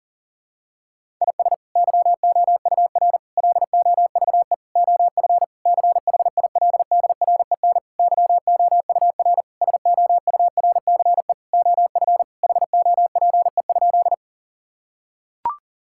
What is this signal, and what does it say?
Morse code, heard as a keyed tone, about 37 wpm, 700 Hz. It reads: IS YOUR LOVE OF CHILDREN YOUR SOURCE OF HOPE?
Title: Sentences from Top 1000 Words - Review Part 5 - 40wpm